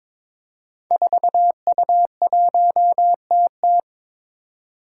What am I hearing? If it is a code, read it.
Morse code, heard as a keyed tone, about 22 wpm, 700 Hz. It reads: 4U1TT
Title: Five Challenging Letter-Number Combinations 22wpm